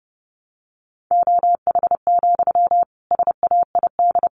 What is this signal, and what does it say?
Morse code, heard as a keyed tone, about 30 wpm, 700 Hz. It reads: OH, SAID
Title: Sentences from Top 500 Words 30wpm